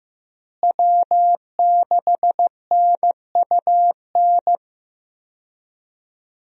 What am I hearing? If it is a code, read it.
Morse code, heard as a keyed tone, about 15 wpm, 700 Hz. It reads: W6NUN
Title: USA Call Signs - 5 Characters - Repeated 2x - 15wpm